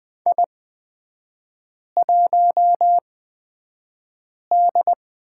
Morse code, heard as a keyed tone, about 20 wpm, spaced as for 5 wpm, 700 Hz. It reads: I1D
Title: 236 Lesson 38e - Letter q - Three Characters 20wpm